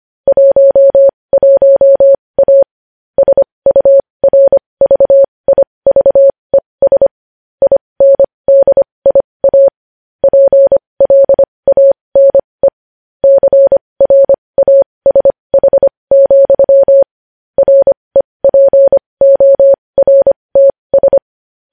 Morse code, heard as a keyed tone, about 25 wpm, spaced as for 20 wpm, 550 Hz. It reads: 11A SURVIVES INDIA PLANE CRASH, REPORTS
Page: news-headlines-in-morse-code-at-20-wpm